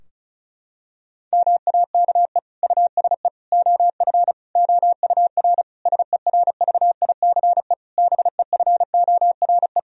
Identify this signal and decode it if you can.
Morse code, heard as a keyed tone, about 34 wpm, 700 Hz. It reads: MAKE USE OF OUR SERVICE BEFORE
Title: Sentences from Top 1000 Words - Review Part 4 - 35wpm